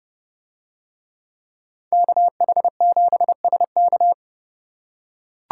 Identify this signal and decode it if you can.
Morse code, heard as a keyed tone, about 30 wpm, 700 Hz. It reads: KH7SK